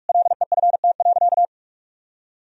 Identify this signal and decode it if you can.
Morse code, heard as a keyed tone, about 42 wpm, 700 Hz. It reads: LEFT.
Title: Sentences from Top 1000 Words - Review Part 2 - 45wpm